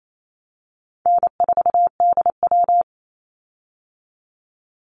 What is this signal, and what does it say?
Morse code, heard as a keyed tone, about 28 wpm, 700 Hz. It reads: N4DW
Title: Call Signs Repeated 2x - 28wpm